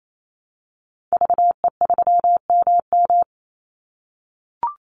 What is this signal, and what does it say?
Morse code, heard as a keyed tone, about 28 wpm, 700 Hz. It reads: VE3MM
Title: Call Signs - 28wpm